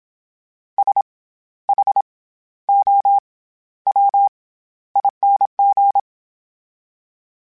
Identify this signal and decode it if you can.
Morse code, heard as a keyed tone, about 27 wpm, 800 Hz. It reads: S H O W ING
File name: Special INGs-mp3s / Special INGs-mp3s / SlowToFaster ing @25